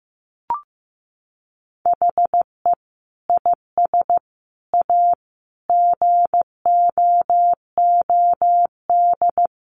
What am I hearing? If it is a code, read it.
Morse code, heard as a keyed tone, about 15 wpm, 700 Hz. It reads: HE IS A GOOD
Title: Sentences from Top 1100 Words 15wpm